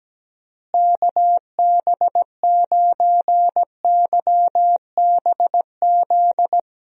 Morse code, heard as a keyed tone, about 17 wpm, 700 Hz. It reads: KB9YBZ
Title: USA Call Signs - 6 Characters - Repeated 3x - 17wpm